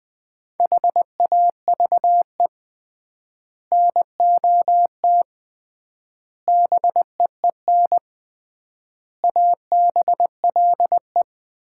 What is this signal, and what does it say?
Morse code, heard as a keyed tone, about 21 wpm, 700 Hz. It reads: HAVE NOT BEEN ABLE
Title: 3x Word-Spacing - Sets of 5 Words - 20wpm